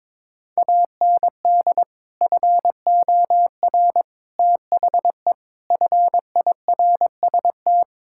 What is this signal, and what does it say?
Morse code, heard as a keyed tone, about 22 wpm, 700 Hz. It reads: AND FOR THE FIRST
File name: practice / 22wpm / Build A Phrase - 5-Word Phrases - 22wpm